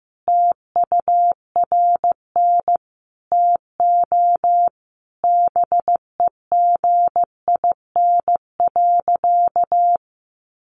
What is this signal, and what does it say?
Morse code, heard as a keyed tone, about 15 wpm, 700 Hz. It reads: TURN TO BEGIN.
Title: Sentences from Top 600 Words 15wpm